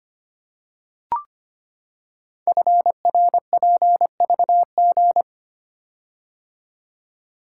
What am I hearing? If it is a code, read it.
Morse code, heard as a keyed tone, about 25 wpm, 700 Hz. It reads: FRPVG
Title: Code Groups - 5 Letters - 25wpm